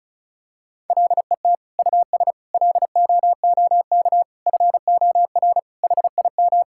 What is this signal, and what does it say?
Morse code, heard as a keyed tone, about 36 wpm, 700 Hz. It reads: LET US LOOK FOR HIM
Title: Sentences from Top 1600 Words 35wpm